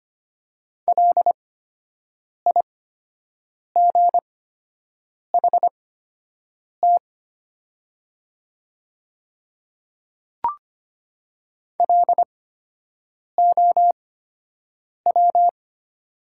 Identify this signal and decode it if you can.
Morse code, heard as a keyed tone, about 25 wpm, spaced as for 7 wpm, 700 Hz. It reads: LIGHT LOW
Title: 8x Character Spacing - Top 200 Words - 25wpm